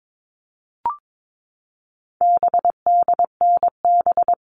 Morse code, heard as a keyed tone, about 22 wpm, 700 Hz. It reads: BDNB
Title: All Cut Numbers - 4-Digit 22wpm